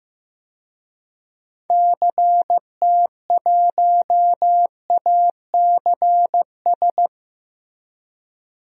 Morse code, heard as a keyed tone, about 15 wpm, 700 Hz. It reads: CT1ACS
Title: DXCC-100 Call Signs - 6 Characters - Repeated 2x - 15wpm